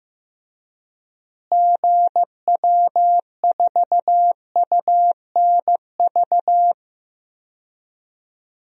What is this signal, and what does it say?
Morse code, heard as a keyed tone, about 15 wpm, 700 Hz. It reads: GW4UNV
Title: DXCC-100 Call Signs - 6 Characters - Repeated 3x - 15wpm